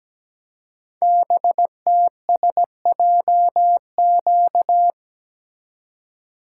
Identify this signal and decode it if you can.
Morse code, heard as a keyed tone, about 17 wpm, 700 Hz. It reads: BTSJQ
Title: Code Groups - 5 Letters - 17wpm